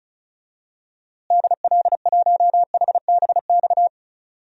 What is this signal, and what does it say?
Morse code, heard as a keyed tone, about 34 wpm, 700 Hz. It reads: DL1HBX
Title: DXCC-100 Call Signs - 6 Characters - Repeated 2x - 35wpm